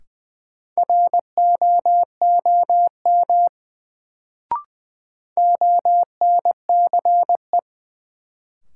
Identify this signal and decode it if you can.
Morse code, heard as a keyed tone, about 21 wpm, 700 Hz. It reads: ROOM ONCE
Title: Sentences from Top 300 Words - New Word Drill - 20wpm